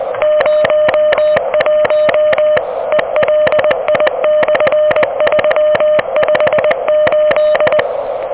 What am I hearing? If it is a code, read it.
Morse code, heard as a keyed tone, about 21 wpm, 600 Hz, with noise. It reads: 01 ELI/358
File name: OK QRP 2023 / OK QRP 2023